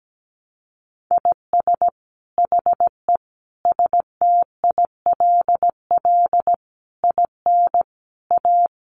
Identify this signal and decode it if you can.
Morse code, heard as a keyed tone, about 17 wpm, 700 Hz. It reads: IS HE STILL IN A
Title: Sentences from Top 800 Words 17wpm